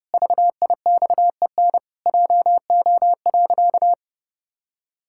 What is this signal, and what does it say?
Morse code, heard as a keyed tone, about 30 wpm, 700 Hz. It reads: VIXEN JO.
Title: Pangram Challenge v2 - 30wpm